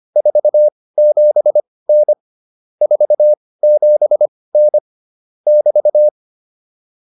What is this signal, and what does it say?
Morse code, heard as a keyed tone, about 25 wpm, spaced as for 18 wpm, 600 Hz. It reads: 47N 47N <BT>
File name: QSO209 18